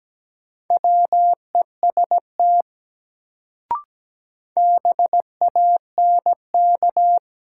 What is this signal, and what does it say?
Morse code, heard as a keyed tone, about 17 wpm, 700 Hz. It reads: WEST BANK
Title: Sets of 5 Words - New Word Drill - 17wpm